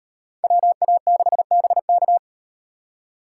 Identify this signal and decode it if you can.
Morse code, heard as a keyed tone, about 38 wpm, 700 Hz. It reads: WA6BK